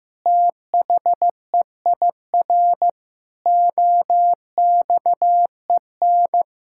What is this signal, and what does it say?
Morse code, heard as a keyed tone, about 15 wpm, 700 Hz. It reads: THEIR OXEN
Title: Pangram Challenge v2 - 15wpm